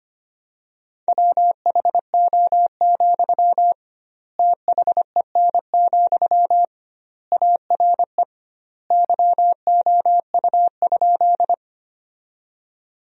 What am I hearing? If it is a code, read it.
Morse code, heard as a keyed tone, about 25 wpm, 700 Hz. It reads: WHO, THEN, ARE YOU?
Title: 2x Word-Spacing - Sentences from Top 100 Words - 25wpm